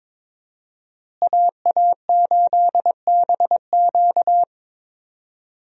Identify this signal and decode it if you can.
Morse code, heard as a keyed tone, about 22 wpm, 690 Hz. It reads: AA8BQ
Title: USA Call Signs - 5 Characters - Repeated 3x - 22wpm